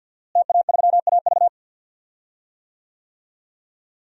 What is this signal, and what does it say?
Morse code, heard as a keyed tone, about 49 wpm, 700 Hz. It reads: TA3AV
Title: DXCC-100 Call Signs - 5 Characters - Repeated 2x - 50wpm